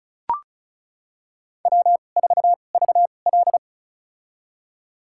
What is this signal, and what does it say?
Morse code, heard as a keyed tone, about 36 wpm, spaced as for 25 wpm, 700 Hz. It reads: W4VL
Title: USA Call Signs - 4 Characters - 2x Character Spacing - 35wpm